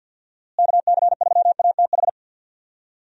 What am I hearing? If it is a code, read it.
Morse code, heard as a keyed tone, about 50 wpm, 700 Hz. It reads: KC3ATH
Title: USA Call Signs - 6 Characters - Repeated 2x - 50wpm